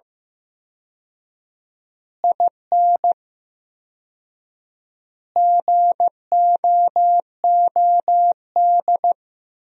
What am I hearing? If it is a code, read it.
Morse code, heard as a keyed tone, about 15 wpm, 700 Hz. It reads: IN GOOD
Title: 4x Word-Spacing - Sentences from Top 200 Words - 15wpm